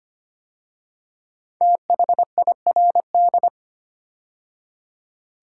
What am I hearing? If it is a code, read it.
Morse code, heard as a keyed tone, about 25 wpm, 700 Hz. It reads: THIRD
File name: 5 Letter Words - First 3 Letters Given - 25wpm